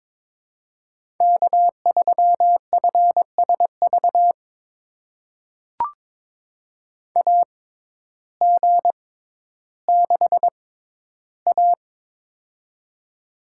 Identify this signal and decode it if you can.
Morse code, heard as a keyed tone, about 22 wpm, spaced as for 12 wpm, 700 Hz. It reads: K3FSV AG6A
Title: USA Call Signs - 6x Character Spacing - 22wpm